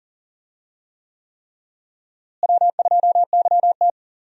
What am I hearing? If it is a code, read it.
Morse code, heard as a keyed tone, about 40 wpm, 700 Hz. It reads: W2YT
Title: USA Call Signs - 4 Characters - Repeated 3x - 40wpm